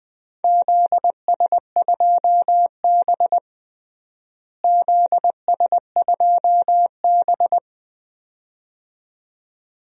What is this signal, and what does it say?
Morse code, heard as a keyed tone, about 20 wpm, 700 Hz. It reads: ZS2B ZS2B
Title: DXCC-100 Call Signs - 4 Characters - Repeated 2x - 20wpm